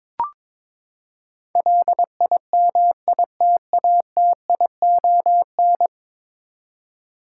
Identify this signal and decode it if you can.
Morse code, heard as a keyed tone, about 22 wpm, 700 Hz. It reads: LIMITATION
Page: morse-code-ninja-practice-22wpm